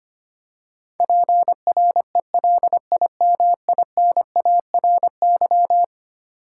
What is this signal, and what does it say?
Morse code, heard as a keyed tone, about 25 wpm, 700 Hz. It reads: PRELIMINARY
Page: morse-code-ninja-practice-25wpm